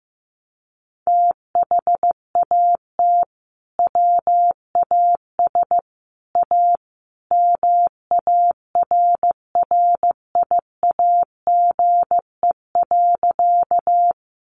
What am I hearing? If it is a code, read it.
Morse code, heard as a keyed tone, about 15 wpm, 700 Hz. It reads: THAT WAS A MARRIAGE.
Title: Sentences from Top 800 Words 15wpm